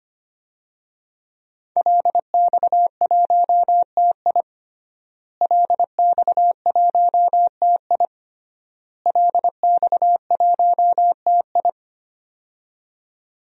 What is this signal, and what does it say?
Morse code, heard as a keyed tone, about 25 wpm, 700 Hz. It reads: LX1TI LX1TI LX1TI
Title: DXCC-100 Call Signs - 5 Characters - Repeated 3x - 25wpm